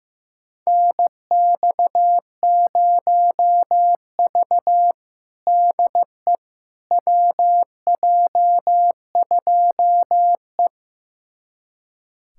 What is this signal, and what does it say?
Morse code, heard as a keyed tone, about 15 wpm, 700 Hz. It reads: NX0V DE WJ2E
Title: USA Callsign DE Callsign - 4 Characters - 15wpm